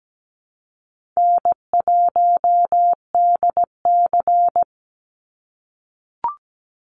Morse code, heard as a keyed tone, about 17 wpm, 700 Hz. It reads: N1DC
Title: Easy Call Signs - 17wpm